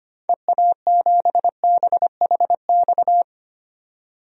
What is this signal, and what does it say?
Morse code, heard as a keyed tone, about 25 wpm, 700 Hz. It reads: EA7BHX
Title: DXCC-100 Call Signs - 6 Characters - 25wpm